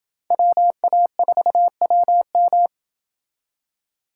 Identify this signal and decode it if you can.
Morse code, heard as a keyed tone, about 27 wpm, 700 Hz. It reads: WA4WM